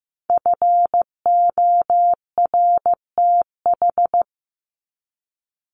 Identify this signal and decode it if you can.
Morse code, heard as a keyed tone, about 15 wpm, 700 Hz. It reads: FORTH
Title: Top 1751-2000 Words 15wpm